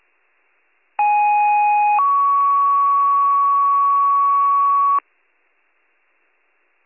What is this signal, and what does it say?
Two-tone page, heard as a one-second tone, then a three-second tone, about 830 then 1120 Hz. It is heard from 1.0 to 5.0 s.